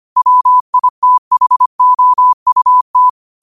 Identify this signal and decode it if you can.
Morse code, heard as a keyed tone, about 25 wpm, 1000 Hz. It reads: WITHOUT
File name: qotdfiles / 25wpm